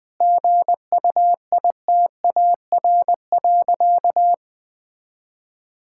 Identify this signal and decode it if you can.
Morse code, heard as a keyed tone, about 21 wpm, 700 Hz. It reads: GUITAR.
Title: Difficult Sentences with Hints - 20wpm